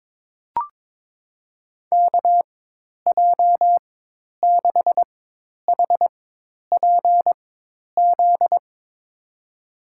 Morse code, heard as a keyed tone, about 22 wpm, spaced as for 10 wpm, 700 Hz. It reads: KJ6HPZ
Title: USA Call Signs - 4x Character Spacing - 22wpm